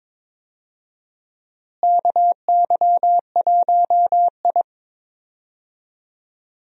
Morse code, heard as a keyed tone, about 22 wpm, 700 Hz. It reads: KY1I
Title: USA Call Signs - 4 Characters - Repeated 2x - 22wpm